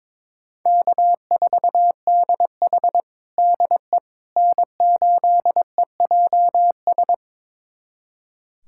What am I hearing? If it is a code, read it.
Morse code, heard as a keyed tone, about 22 wpm, 700 Hz. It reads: K4DH DE N8EJS
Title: USA Callsign DE Callsign 22wpm